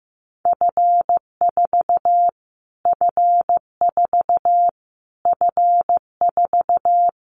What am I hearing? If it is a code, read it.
Morse code, heard as a keyed tone, about 15 wpm, 700 Hz. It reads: F4 F4 F4
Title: Callsign Prefixes for Top 25 DXCC - Repeated 3x - 15wpm